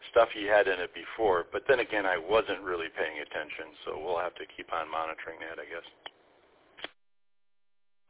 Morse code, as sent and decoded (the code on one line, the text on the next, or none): none